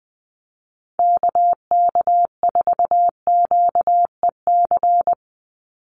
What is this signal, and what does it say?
Morse code, heard as a keyed tone, about 20 wpm, 700 Hz. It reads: KK4QEC
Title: USA Call Signs 20wpm